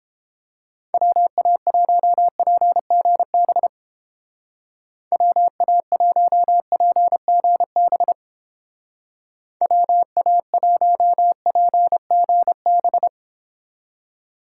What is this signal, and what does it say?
Morse code, heard as a keyed tone, about 31 wpm, 700 Hz. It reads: WA1PGB WA1PGB WA1PGB